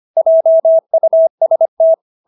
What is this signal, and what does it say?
Morse code, heard as a keyed tone, about 25 wpm, 650 Hz. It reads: JUST